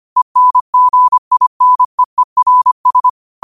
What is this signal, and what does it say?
Morse code, heard as a keyed tone, about 25 wpm, 1000 Hz. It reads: ENGINEERS
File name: qotdfiles / 25wpm